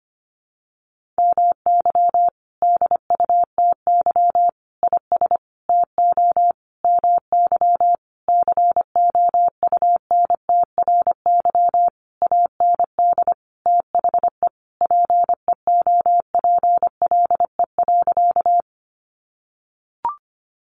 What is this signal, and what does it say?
Morse code, heard as a keyed tone, about 25 wpm, 700 Hz. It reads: MY DUTY IS TO MY COUNTRY AND THE PEOPLE.
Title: Sentences from Top 1500 Words 25wpm